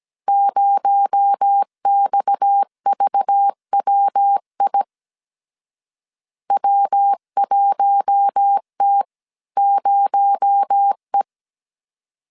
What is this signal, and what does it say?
Morse code, heard as a keyed tone, about 17 wpm, 800 Hz. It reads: 0XVWI W1T0E
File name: CW / DET / 1 6 1500C S A 16wpm 2635